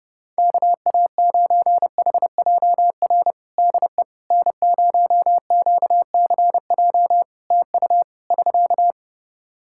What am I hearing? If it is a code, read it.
Morse code, heard as a keyed tone, about 30 wpm, 700 Hz. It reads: KA9HJR DE N0QCJ TU <SK>
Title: Practice QSO - Exchange 4 - Ending - 30wpm